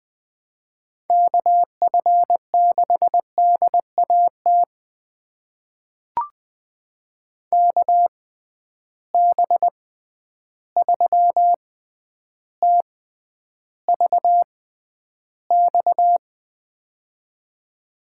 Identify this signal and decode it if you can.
Morse code, heard as a keyed tone, about 20 wpm, spaced as for 10 wpm, 700 Hz. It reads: KF6DAT KB3TVX
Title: USA Call Signs - 6 Characters - 6x Character Spacing - 20wpm